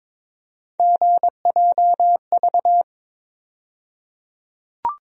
Morse code, heard as a keyed tone, about 23 wpm, 700 Hz. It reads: GJV
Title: USA Call Sign Postfixes - 3 Letters 22wpm